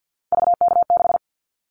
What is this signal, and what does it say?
Morse code, heard as a keyed tone, about 50 wpm, 700 Hz. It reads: VK6
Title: Callsign Prefixes for Top 50 DXCC - 50wpm